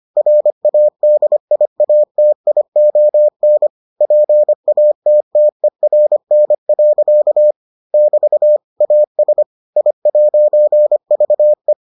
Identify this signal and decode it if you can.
Morse code, heard as a keyed tone, about 25 wpm, 600 Hz. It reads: RADIATION PATTERN. <BT> AS I'VE